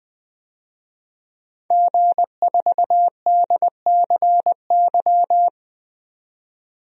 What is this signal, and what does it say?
Morse code, heard as a keyed tone, about 20 wpm, 700 Hz. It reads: G4DCY
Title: DXCC-100 Call Signs - 5 Characters - 20wpm